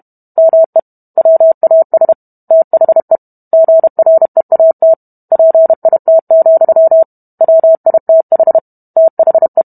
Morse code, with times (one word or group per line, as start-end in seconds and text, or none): 0.4-0.8 ME
1.2-2.1 WAS
2.5-3.2 THE
3.5-4.9 GREAT
5.3-7.0 PIT,
7.4-8.6 WITH
9.0-9.6 THE